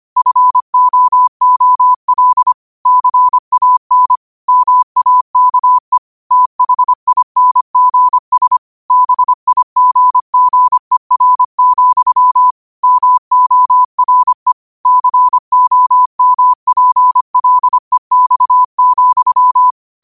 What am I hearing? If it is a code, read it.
Morse code, heard as a keyed tone, about 25 wpm, 1000 Hz. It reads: FOOL CAN MAKE THINGS BIGGER, MORE COMPLEX,
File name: qotdfiles / 25wpm